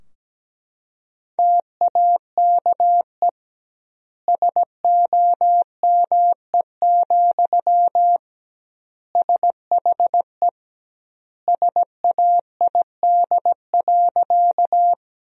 Morse code, heard as a keyed tone, about 17 wpm, 700 Hz. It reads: TAKE SOME, SHE SAID.
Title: 2x Word-Spacing - Sentences from Top 200 Words - 17wpm